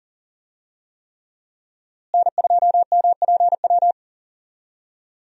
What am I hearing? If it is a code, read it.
Morse code, heard as a keyed tone, about 40 wpm, 700 Hz. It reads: N2MPW
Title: USA Call Signs - 5 Characters - Repeated 3x - 40wpm